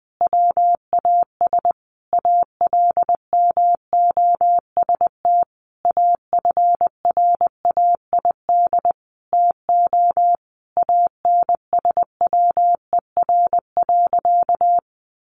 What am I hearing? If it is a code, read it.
Morse code, heard as a keyed tone, about 20 wpm, 700 Hz. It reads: WAS ALMOST AFRAID TO ANSWER.
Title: Sentences from Top 1500 Words 20wpm